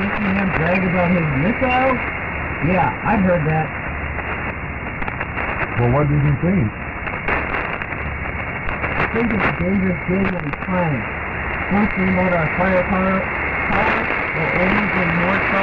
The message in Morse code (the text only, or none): none